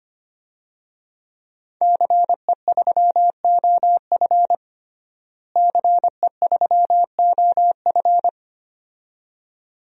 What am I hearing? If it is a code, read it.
Morse code, heard as a keyed tone, about 25 wpm, 700 Hz. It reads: CE3OF CE3OF
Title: DXCC-100 Call Signs - 5 Characters - Repeated 2x - 25wpm